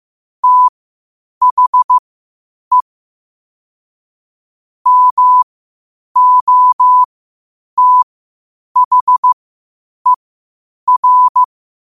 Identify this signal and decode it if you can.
Morse code, heard as a keyed tone, about 15 wpm, spaced as for 8 wpm, 1000 Hz. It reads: THE MOTHER
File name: qotdfiles / 8wpm